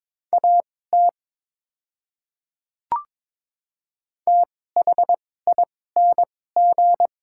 Morse code, heard as a keyed tone, about 22 wpm, spaced as for 16 wpm, 700 Hz. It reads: AT THING
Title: 2x Character Spacing - Top 100 Words - 22wpm